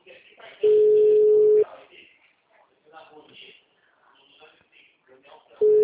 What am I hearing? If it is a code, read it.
Ringback tone (Europe, 425 Hz), with speech.
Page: TLML PRUDENTE 1